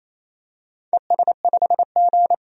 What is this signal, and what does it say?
Morse code, heard as a keyed tone, about 28 wpm, 700 Hz. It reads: ES5G
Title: DXCC-100 Call Signs - 4 Characters - 28wpm